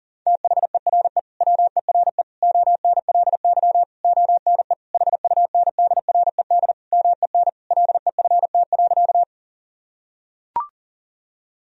Morse code, heard as a keyed tone, about 37 wpm, 700 Hz. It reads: THERE WERE ONLY ONE HUNDRED MEN LEFT.
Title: Sentences from Top 1000 Words - Review Part 2 - 40wpm